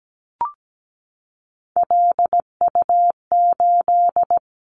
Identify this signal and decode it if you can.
Morse code, heard as a keyed tone, about 17 wpm, 700 Hz. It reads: LU8
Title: Callsign Prefixes for Top 100 DXCC - 17wpm